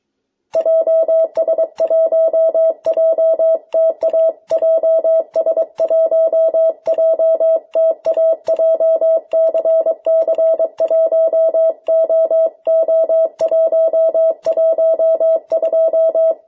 Morse code, heard as a keyed tone, about 22 wpm, 650 Hz. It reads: JS1JTA JS1JTA JCC1OO112
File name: jh8cbh / 2021cw